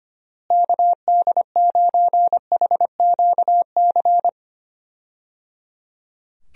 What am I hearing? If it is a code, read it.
Morse code, heard as a keyed tone, about 25 wpm, 700 Hz. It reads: KD9HQC
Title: USA Call Signs - 6 Characters - Repeated 3x - 25wpm